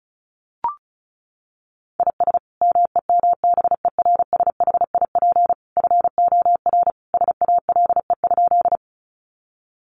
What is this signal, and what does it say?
Morse code, heard as a keyed tone, about 34 wpm, 700 Hz. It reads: IS MEMBERSHIP FOR SALE?
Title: Sentences from Top 1400 Words 35wpm